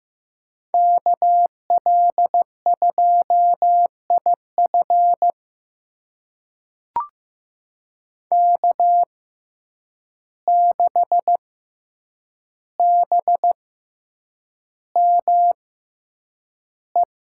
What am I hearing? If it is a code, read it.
Morse code, heard as a keyed tone, about 15 wpm, spaced as for 8 wpm, 700 Hz. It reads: KL2IF K6BME
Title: USA Call Signs - 5 Characters - 6x Character Spacing - 15wpm